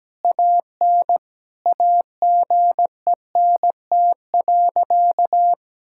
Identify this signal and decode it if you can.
Morse code, heard as a keyed tone, about 17 wpm, 700 Hz. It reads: AN AGENT.
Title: Sentences from Top 1600 Words 17wpm